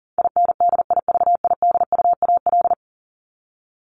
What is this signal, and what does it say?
Morse code, heard as a keyed tone, about 40 wpm, 700 Hz. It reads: INDIVIDUAL
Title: ES:S Suffix - Top 100 Words with Base Words 40wpm